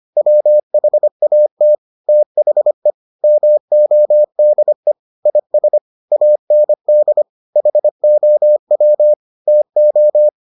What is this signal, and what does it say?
Morse code, heard as a keyed tone, about 25 wpm, 600 Hz. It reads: WHAT THE MODE IS AND HOW TO